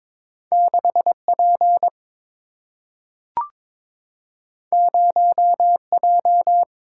Morse code, heard as a keyed tone, about 22 wpm, 700 Hz. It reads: 6 P 0 J
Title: Single-Letter Single-Number 22wpm